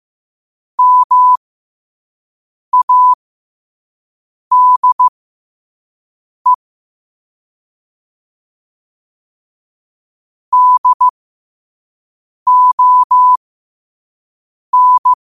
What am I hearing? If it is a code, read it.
Morse code, heard as a keyed tone, about 15 wpm, spaced as for 5 wpm, 1000 Hz. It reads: MADE DON
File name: qotdfiles / 5wpm